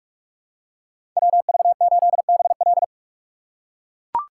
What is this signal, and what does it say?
Morse code, heard as a keyed tone, about 42 wpm, 690 Hz. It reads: WV8BL
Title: USA Call Signs - 5 Characters - 6x Character Spacing - 45wpm